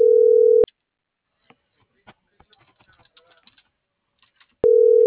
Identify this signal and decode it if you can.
Ringback tone (North America), 440 + 480 Hz.